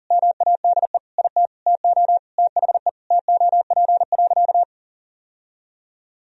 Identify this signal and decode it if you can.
Morse code, heard as a keyed tone, about 40 wpm, 700 Hz. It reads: MADE IT TO THE TOP.
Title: Sentences from Top 1000 Words - Review Part 2 - 40wpm